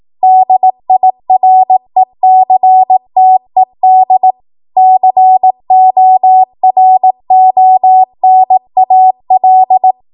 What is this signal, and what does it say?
Morse code, heard as a keyed tone, about 18 wpm, 750 Hz. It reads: DIRECTED CORONAL
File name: Morse / 18WPM